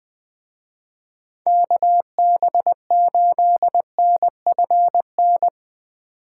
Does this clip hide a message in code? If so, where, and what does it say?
Yes: Morse code from 1.5 to 5.5 s, heard as a keyed tone, about 20 wpm, 700 Hz. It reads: KB8NFN